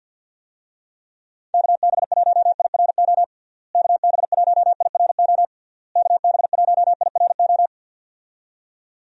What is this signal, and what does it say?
Morse code, heard as a keyed tone, about 52 wpm, 700 Hz. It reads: KB1IRO KB1IRO KB1IRO
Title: USA Call Signs - 6 Characters - Repeated 3x - 50wpm